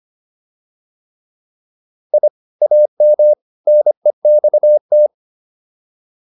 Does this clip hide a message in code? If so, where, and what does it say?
Yes: Morse code from 2.1 to 5.1 s, heard as a keyed tone, about 25 wpm, 600 Hz. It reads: I AM NEXT